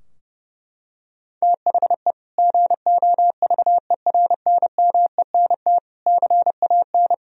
Morse code, heard as a keyed tone, about 30 wpm, 700 Hz. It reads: THE GOVERNMENT CAN
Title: Sentences from Top 1000 Words - Review Part 3 - 30wpm